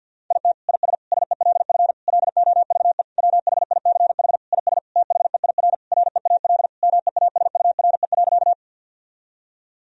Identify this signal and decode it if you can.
Morse code, heard as a keyed tone, about 48 wpm, 700 Hz. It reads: IT IS SELF LOVE WHICH IS THEIR REAL MEASURE.
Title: Sentences from Top 1000 Words - Review Part 1 - 50wpm